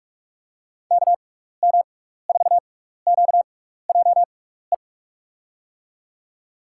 Morse code, heard as a keyed tone, about 42 wpm, spaced as for 15 wpm, 700 Hz. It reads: KM4QJE
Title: USA Call Signs - 6 Characters - 6x Character Spacing - 45wpm